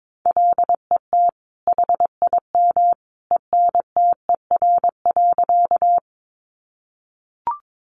Morse code, heard as a keyed tone, about 22 wpm, 700 Hz. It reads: LET HIM ENTER.
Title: Sentences from Top 1300 Words 22wpm